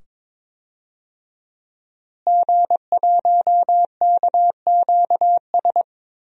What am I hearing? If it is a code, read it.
Morse code, heard as a keyed tone, about 22 wpm, 700 Hz. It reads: G1KQS
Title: DXCC-100 Call Signs - 5 Characters - Repeated 2x - 22wpm